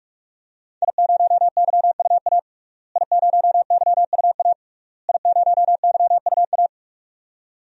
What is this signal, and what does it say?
Morse code, heard as a keyed tone, about 41 wpm, 700 Hz. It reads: I0YUA I0YUA I0YUA
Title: DXCC-100 Call Signs - 5 Characters - Repeated 3x - 45wpm